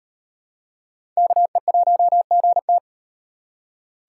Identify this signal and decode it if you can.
Morse code, heard as a keyed tone, about 38 wpm, 700 Hz. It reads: KE1GT